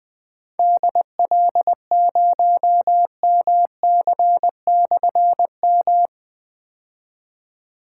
Morse code, heard as a keyed tone, about 20 wpm, 700 Hz. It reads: DL0MC/M
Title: Difficult Call Signs 20wpm